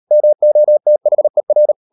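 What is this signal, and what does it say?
Morse code, heard as a keyed tone, about 38 wpm, 600 Hz. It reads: MOTHER
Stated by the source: Morse code practice at 38 WPM, sending Ken Follett's novel Never in English, from CW med Gnister.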